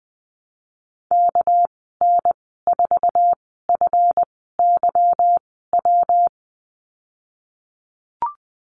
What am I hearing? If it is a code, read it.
Morse code, heard as a keyed tone, about 19 wpm, spaced as for 14 wpm, 700 Hz. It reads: KN4FYW
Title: 213 Lesson 34g - Number 8 - Call Signs 20wpm